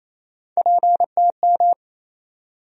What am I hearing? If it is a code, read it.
Morse code, heard as a keyed tone, about 28 wpm, 700 Hz. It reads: PTM
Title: USA Call Sign Postfixes - 3 Letters 28wpm